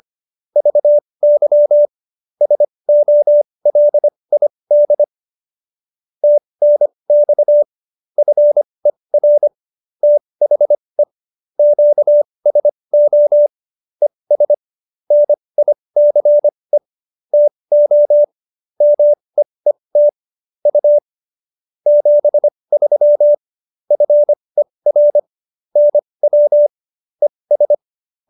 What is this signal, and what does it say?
Morse code, heard as a keyed tone, about 25 wpm, spaced as for 20 wpm, 590 Hz. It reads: VY SOLID TNX FER THE QSO ES NICE TO MEET U 73 FER NW ES
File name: qso220b 20